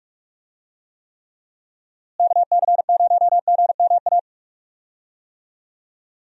Morse code, heard as a keyed tone, about 46 wpm, 700 Hz. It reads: KC0GMA
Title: USA Call Signs - 6 Characters - Repeated 2x - 45wpm